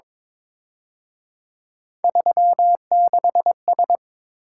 Morse code, heard as a keyed tone, about 26 wpm, 700 Hz. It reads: 36S
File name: Speed-Racing - Three Letter-Number Combinations - 22wpm